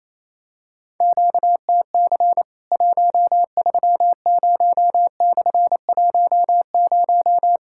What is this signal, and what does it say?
Morse code, heard as a keyed tone, about 28 wpm, 700 Hz. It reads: QTC 130/10
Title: Worked All Europe DX Contest - QTC Exchange - 28wpm